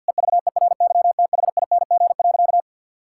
Morse code, heard as a keyed tone, about 50 wpm, 700 Hz. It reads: EVERYTHING.
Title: Sentences from Top 1000 Words - Review Part 5 - 50wpm